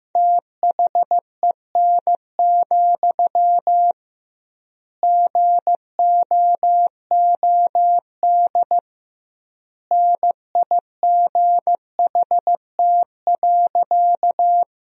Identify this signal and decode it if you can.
Morse code, heard as a keyed tone, about 15 wpm, 700 Hz. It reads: THEN, GOOD NIGHT.
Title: 2x Word-Spacing - Sentences from Top 300 Words - 15wpm